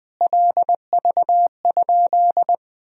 Morse code, heard as a keyed tone, about 21 wpm, 700 Hz. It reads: LV?
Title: Sentences from Top 400 Words and Abbreviations 20wpm